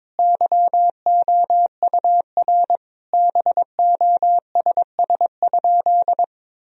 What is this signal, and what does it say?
Morse code, heard as a keyed tone, about 21 wpm, 700 Hz. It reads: YOUR BOSS?
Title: Sentences from Top 1000 Words - Review Part 4 - 22wpm